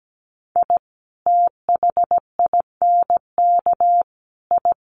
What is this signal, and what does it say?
Morse code, heard as a keyed tone, about 17 wpm, 700 Hz. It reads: I THINK I
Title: Sets of 3 Words 17wpm